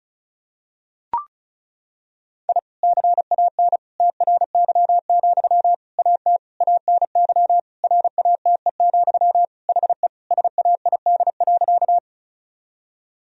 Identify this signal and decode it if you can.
Morse code, heard as a keyed tone, about 36 wpm, 700 Hz. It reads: I CAN TRY, AT ANY RATE, HE SAID.
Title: Sentences from Top 1000 Words - Review Part 4 - 35wpm